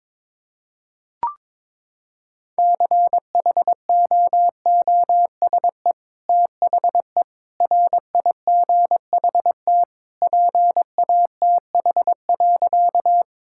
Morse code, heard as a keyed tone, about 22 wpm, 700 Hz. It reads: CHOOSE THE RIGHT PATH.
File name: practice / 22wpm / Sentences with 2-Word Phrases - 22wpm